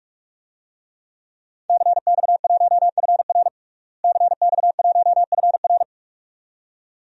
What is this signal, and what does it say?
Morse code, heard as a keyed tone, about 42 wpm, 700 Hz. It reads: CX1FR CX1FR
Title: DXCC-100 Call Signs - 5 Characters - Repeated 2x - 45wpm